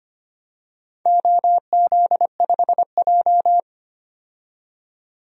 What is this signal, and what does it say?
Morse code, heard as a keyed tone, about 25 wpm, 700 Hz. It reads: OZ5J
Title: DXCC-100 Call Signs - 4 Characters - 25wpm